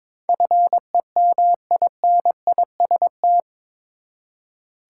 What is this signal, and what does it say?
Morse code, heard as a keyed tone, about 22 wpm, 700 Hz. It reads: FEMINIST
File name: practice / 22wpm / Eight Letter Words - 22wpm